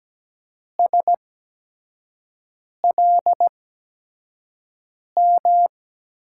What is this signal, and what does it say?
Morse code, heard as a keyed tone, about 17 wpm, spaced as for 5 wpm, 700 Hz. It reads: SLM
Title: USA Call Signs - 6 Characters - 8x Character Spacing - 17wpm